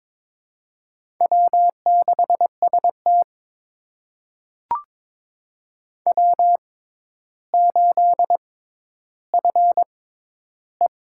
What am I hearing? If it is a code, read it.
Morse code, heard as a keyed tone, about 21 wpm, spaced as for 11 wpm, 700 Hz. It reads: W6ST W8FE
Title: USA Call Signs - 4 Characters - 6x Character Spacing - 22wpm